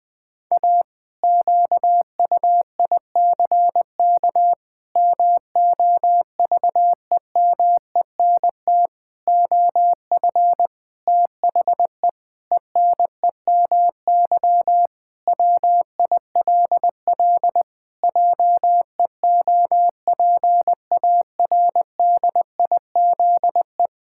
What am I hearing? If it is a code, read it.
Morse code, heard as a keyed tone, about 20 wpm, 700 Hz. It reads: A QUICK MOVEMENT OF THE ENEMY WILL JEOPARDIZE
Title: Pangram Challenge v2 - 20wpm